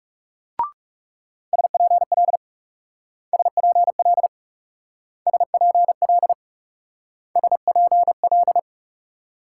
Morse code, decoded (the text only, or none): SPL SPL SPL SPL